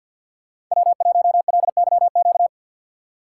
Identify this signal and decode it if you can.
Morse code, heard as a keyed tone, about 52 wpm, 700 Hz. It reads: W1LYQ
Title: USA Call Signs - 5 Characters - 4x Character Spacing - 50wpm